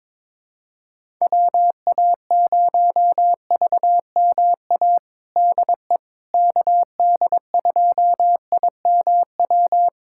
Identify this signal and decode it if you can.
Morse code, heard as a keyed tone, about 22 wpm, 700 Hz. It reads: WA0VMA DE KD2IMW
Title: USA Callsign DE Callsign 22wpm